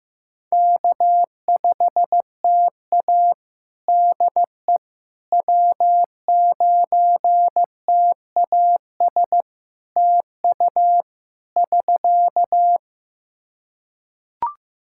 Morse code, heard as a keyed tone, about 16 wpm, 700 Hz. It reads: K5TA DE W9TAS TU <SK>
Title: Practice QSO - Exchange 4 - Ending - 15wpm